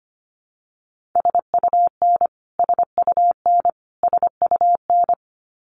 Morse code, heard as a keyed tone, about 25 wpm, 700 Hz. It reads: SUN SUN SUN
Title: Top 500 Words Repeated 25wpm